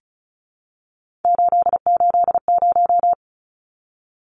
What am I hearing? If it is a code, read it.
Morse code, heard as a keyed tone, about 34 wpm, 700 Hz. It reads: 880
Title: Numbers - 3-Digit 35wpm